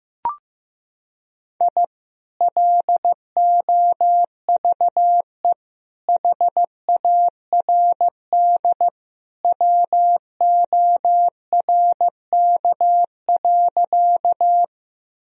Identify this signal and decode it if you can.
Morse code, heard as a keyed tone, about 15 wpm, 700 Hz. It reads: I LOVE HARD WORK.
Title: Sentences from Top 1000 Words - Review Part 1 - 15wpm